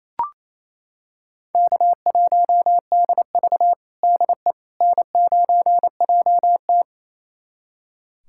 Morse code, heard as a keyed tone, about 28 wpm, 700 Hz. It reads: K1DV DE N9JT
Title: USA Callsign DE Callsign - 4 Characters - 28wpm